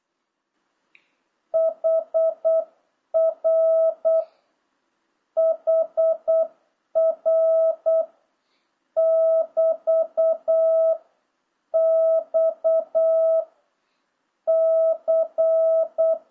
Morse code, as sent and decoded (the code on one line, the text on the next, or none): .... .-. / .... .-. / -...- / -..- / -.-.
HR HR <BT> X C